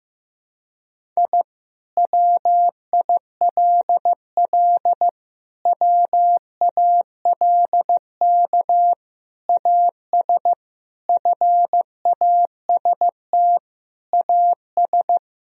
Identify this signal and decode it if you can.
Morse code, heard as a keyed tone, about 15 wpm, 700 Hz. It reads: I WILL WALK AS FAST AS
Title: Sentences from Top 1000 Words - Review Part 2 - 15wpm